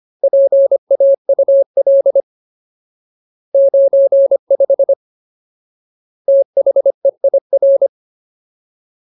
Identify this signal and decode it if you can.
Morse code, heard as a keyed tone, about 25 wpm, 550 Hz. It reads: PAUL 95 THEIR
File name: Ultimate Word List / 25WPM Ultimate